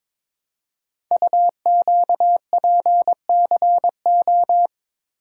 Morse code, heard as a keyed tone, about 22 wpm, 700 Hz. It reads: UQPCO
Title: Code Groups - 5 Letters - 22wpm